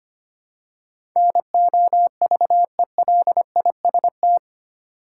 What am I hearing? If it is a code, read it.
Morse code, heard as a keyed tone, about 25 wpm, 700 Hz. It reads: NOVELIST
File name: 8 Letter Words - First 3 Letters Given - 25wpm